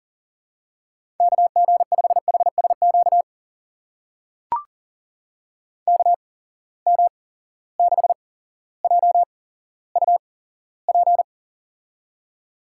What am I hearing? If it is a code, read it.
Morse code, heard as a keyed tone, about 40 wpm, spaced as for 17 wpm, 700 Hz. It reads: KG5HSQ KM6JUP
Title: USA Call Signs - 6 Characters - 8x Character Spacing - 40wpm